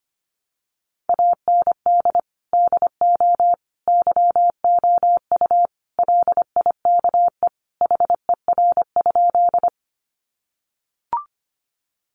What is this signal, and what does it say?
Morse code, heard as a keyed tone, about 25 wpm, 700 Hz. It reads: AND DO YOU LIKE HER?
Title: Sentences from Top 100 Words 25wpm